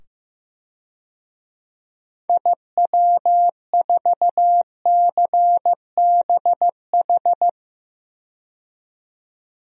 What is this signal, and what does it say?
Morse code, heard as a keyed tone, about 15 wpm, 700 Hz. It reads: IW4CBH
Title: DXCC-100 Call Signs - 6 Characters - 15wpm